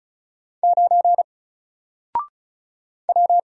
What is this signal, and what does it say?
Morse code, heard as a keyed tone, about 35 wpm, 700 Hz. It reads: 9 W
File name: Single-Letter-Number-35wpm